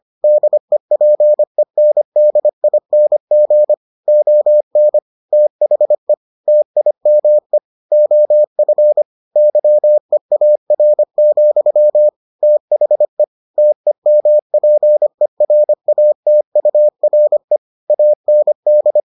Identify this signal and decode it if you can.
Morse code, heard as a keyed tone, about 25 wpm, 600 Hz. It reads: DEPENDING ON THE TIME OF YEAR, THE TEMPERATURE AND